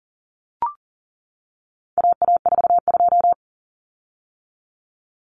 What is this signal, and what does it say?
Morse code, heard as a keyed tone, about 40 wpm, 700 Hz. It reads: AA42
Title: Common Cut Numbers - 4-Digit 40wpm